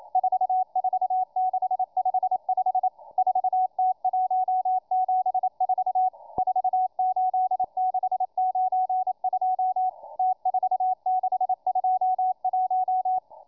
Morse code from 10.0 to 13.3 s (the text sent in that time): T4621